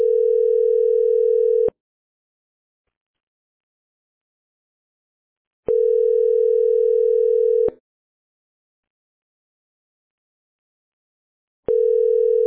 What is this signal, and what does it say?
Ringback tone (North America), 440 + 480 Hz.